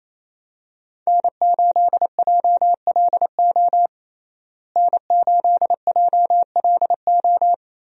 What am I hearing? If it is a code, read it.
Morse code, heard as a keyed tone, about 28 wpm, 700 Hz. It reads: N8JLO N8JLO
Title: USA Call Signs - 5 Characters - Repeated 3x - 28wpm